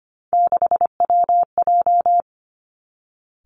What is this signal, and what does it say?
Morse code, heard as a keyed tone, about 25 wpm, 700 Hz. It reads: 6WJ
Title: Three Challenging Letter-Number Combinations 25wpm